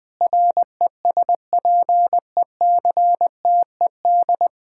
Morse code, heard as a keyed tone, about 20 wpm, 700 Hz. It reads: RESPECTED
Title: Sentences with 6-Word Phrases - 20wpm